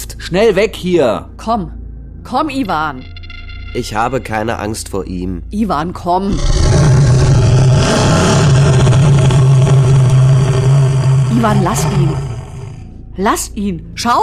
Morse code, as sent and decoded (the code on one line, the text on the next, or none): none